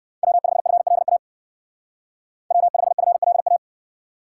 Morse code, heard as a keyed tone, about 58 wpm, 700 Hz. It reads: W5FLA W5FLA